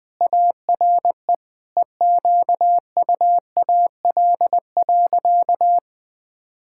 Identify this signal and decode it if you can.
Morse code, heard as a keyed tone, about 20 wpm, 700 Hz. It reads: ARE EQUAL.
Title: Sentences with 9-Word Phrases - 20wpm